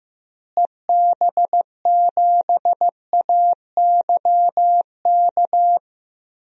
Morse code, heard as a keyed tone, about 15 wpm, 700 Hz. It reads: EB7AYK
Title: DXCC-100 Call Signs - 6 Characters - Repeated 3x - 15wpm